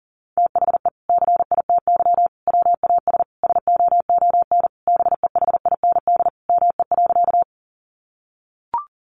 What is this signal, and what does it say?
Morse code, heard as a keyed tone, about 40 wpm, 700 Hz. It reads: THE CITY WAS SOON BEHIND ME.
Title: Sentences from Top 500 Words 40wpm